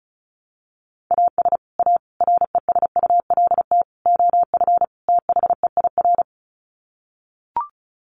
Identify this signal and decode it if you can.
Morse code, heard as a keyed tone, about 34 wpm, 700 Hz. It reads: AS A RESULT OF THEIR
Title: Sets of 5 Words Encore2 35wpm